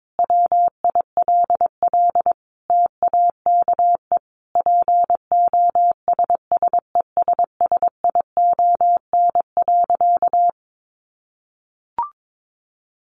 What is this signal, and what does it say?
Morse code, heard as a keyed tone, about 22 wpm, 700 Hz. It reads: WILL TAKE POSSESSION.
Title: Sentences from Top 1400 Words 22wpm